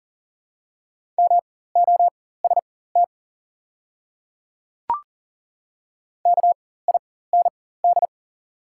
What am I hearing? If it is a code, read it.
Morse code, heard as a keyed tone, about 40 wpm, spaced as for 19 wpm, 700 Hz. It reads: MOST KIND
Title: 4x Character Spacing - Top 200 Words - 40wpm